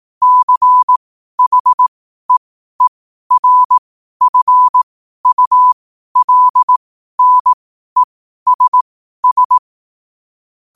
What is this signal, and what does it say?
Morse code, heard as a keyed tone, about 18 wpm, spaced as for 12 wpm, 1000 Hz. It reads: CHEERFULNESS
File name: qotdfiles / 12wpm